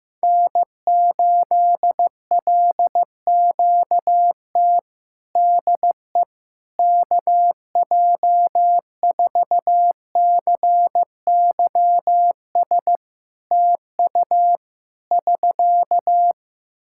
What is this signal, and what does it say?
Morse code, heard as a keyed tone, about 15 wpm, 700 Hz. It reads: N8LQT DE KJ4CYS TU <SK>
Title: K1USN SST QSO Practice 15wpm